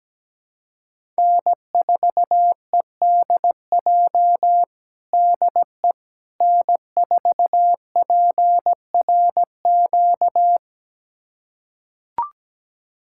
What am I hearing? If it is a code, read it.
Morse code, heard as a keyed tone, about 17 wpm, 700 Hz. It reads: N4EDJ DE N4PRQ
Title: USA Callsign DE Callsign - 5 Characters - 17wpm